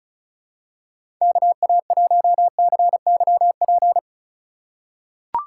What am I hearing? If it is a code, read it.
Morse code, heard as a keyed tone, about 34 wpm, 700 Hz. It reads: KA1CYP
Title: USA Call Signs - 3x Character Spacing - 35wpm